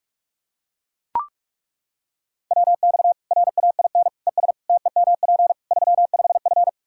Morse code, heard as a keyed tone, about 42 wpm, 700 Hz. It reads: WX RAIN ES TEMP 35F
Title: Practice QSO - Exchange 2 - Info - 45wpm